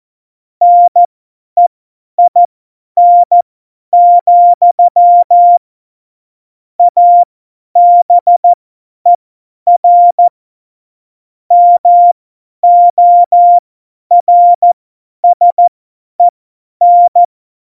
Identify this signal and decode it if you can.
Morse code, heard as a keyed tone, about 14 wpm, spaced as for 10 wpm, 700 Hz. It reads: NEIN, ABER MORSEN
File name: Morsekurs / Witze / Witz 06. November 2018, 50 F70